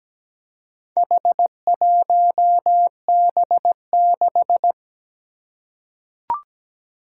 Morse code, heard as a keyed tone, about 17 wpm, 700 Hz. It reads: H1B6
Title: Four Challenging Letter-Number Combinations 17wpm